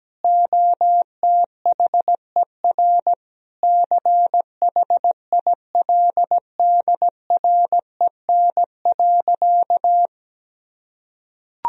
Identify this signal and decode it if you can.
Morse code, heard as a keyed tone, about 17 wpm, 700 Hz. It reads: OTHER CHILDREN.
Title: Sentences from Top 1000 Words - Review Part 1 - 17wpm